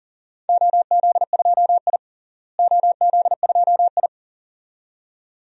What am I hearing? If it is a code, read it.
Morse code, heard as a keyed tone, about 40 wpm, 700 Hz. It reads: OZ2I OZ2I
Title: DXCC-100 Call Signs - 4 Characters - Repeated 2x - 40wpm